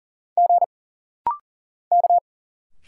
Morse code, heard as a keyed tone, about 40 wpm, 700 Hz. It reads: G K
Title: Single Letters - Rapid-Fire 40wpm